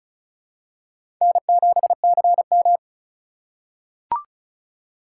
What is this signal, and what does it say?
Morse code, heard as a keyed tone, about 36 wpm, 700 Hz. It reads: N7CM
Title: USA Call Signs - 4 Characters - 2x Character Spacing - 35wpm